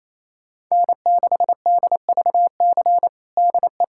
Morse code, heard as a keyed tone, about 28 wpm, 700 Hz. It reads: N6DVC DE
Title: USA Callsign DE Callsign 28wpm